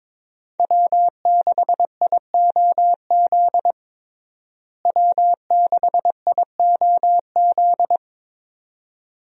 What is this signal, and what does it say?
Morse code, heard as a keyed tone, about 22 wpm, 700 Hz. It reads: W6IOZ W6IOZ
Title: USA Call Signs - 5 Characters - Repeated 2x - 22wpm